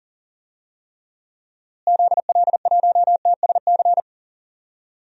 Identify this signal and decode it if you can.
Morse code, heard as a keyed tone, about 37 wpm, 700 Hz. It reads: ZL1TSC
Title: DXCC-100 Call Signs - 6 Characters - Repeated 3x - 40wpm